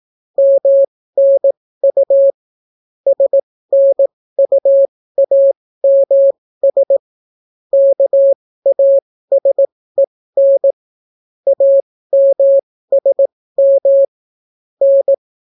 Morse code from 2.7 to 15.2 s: SNUAMS KASEN AMSM N